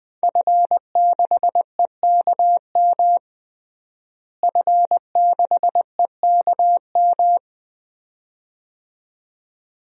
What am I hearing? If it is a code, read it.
Morse code, heard as a keyed tone, about 20 wpm, 700 Hz. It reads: F6EKM F6EKM
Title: DXCC-100 Call Signs - 5 Characters - Repeated 2x - 20wpm